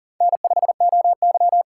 Morse code, heard as a keyed tone, about 37 wpm, 700 Hz. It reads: N5OY
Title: USA Call Signs - 2x Character Spacing - 40wpm